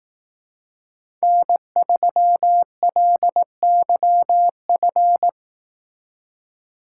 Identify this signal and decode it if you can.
Morse code, heard as a keyed tone, about 18 wpm, 700 Hz. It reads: N3LYF
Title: Speed Racing - USA 1x3 Call Signs 15wpm